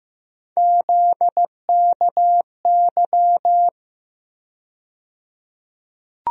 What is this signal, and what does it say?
Morse code, heard as a keyed tone, about 15 wpm, 700 Hz. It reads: ZKY
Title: USA Call Sign Postfixes - 3 Letters 15wpm